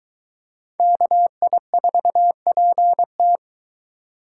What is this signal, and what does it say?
Morse code, heard as a keyed tone, about 23 wpm, 700 Hz. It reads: KI4PT